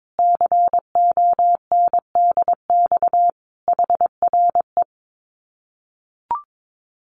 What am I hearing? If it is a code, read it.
Morse code, heard as a keyed tone, about 22 wpm, 700 Hz. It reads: CONDX HRE
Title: Common QSO Phrases 22wpm